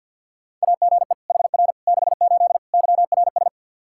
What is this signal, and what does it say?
Morse code, heard as a keyed tone, about 47 wpm, 700 Hz. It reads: AGE HR 68 YRS
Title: K1USN SST QSO Practice 50wpm